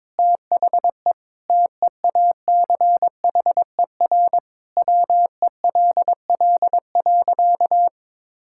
Morse code, heard as a keyed tone, about 22 wpm, 700 Hz. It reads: THE TEACHER WELL.
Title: Sentences from Top 1000 Words - Review Part 6 - 22wpm